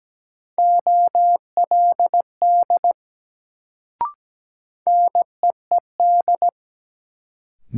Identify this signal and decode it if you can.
Morse code, heard as a keyed tone, about 17 wpm, 700 Hz. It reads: OLD NEED
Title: Sentences from Top 200 Words - New Word Drill - 17wpm